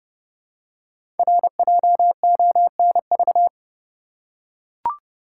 Morse code, heard as a keyed tone, about 28 wpm, 700 Hz. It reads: RJONV
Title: Code Groups - 5 Letters - 30wpm